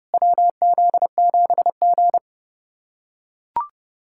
Morse code, heard as a keyed tone, about 29 wpm, 690 Hz. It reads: WZ7G
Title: SOTA QSO Practice 30wpm